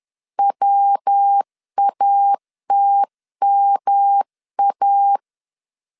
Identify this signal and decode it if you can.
Morse code, heard as a keyed tone, about 10 wpm, 800 Hz. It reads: WATMA
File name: 10wpm